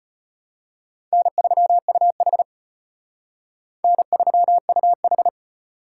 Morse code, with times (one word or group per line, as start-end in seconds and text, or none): 1.1-2.4 N3UH
3.8-5.3 N3UH